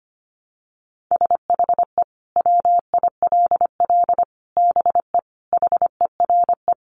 Morse code, heard as a keyed tone, about 25 wpm, 700 Hz. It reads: SHE WILL BE HERE
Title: Sentences from Top 600 Words 25wpm